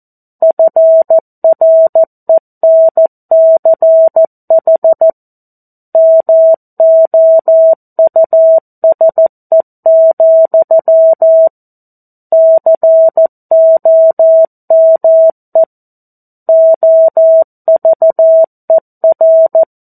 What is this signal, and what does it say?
Morse code, heard as a keyed tone, about 14 wpm, 650 Hz. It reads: FRENCH MOUSE, COME OVER
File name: 02-Chapter-02 14 WPM